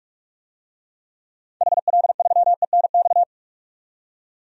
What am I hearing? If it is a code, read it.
Morse code, heard as a keyed tone, about 46 wpm, 700 Hz. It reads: HL3ENX
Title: DXCC-100 Call Signs - 6 Characters - 45wpm